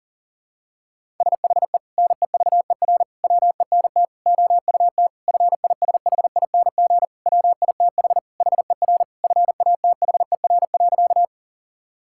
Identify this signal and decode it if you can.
Morse code, heard as a keyed tone, about 40 wpm, 700 Hz. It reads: SHE NEVER WENT OUT FISHING WITH HER FATHER.
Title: Sentences from Top 1000 Words - Review Part 5 - 40wpm